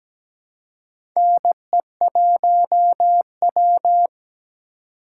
Morse code, heard as a keyed tone, about 17 wpm, 700 Hz. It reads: NE1W
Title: USA Call Signs - 4 Characters - 3x Character Spacing - 17wpm